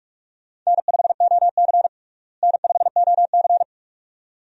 Morse code, heard as a keyed tone, about 42 wpm, 700 Hz. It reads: N5OC N5OC
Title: USA Call Signs - 4 Characters - Repeated 2x - 45wpm